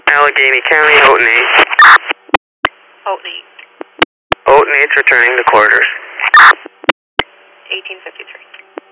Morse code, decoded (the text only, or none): none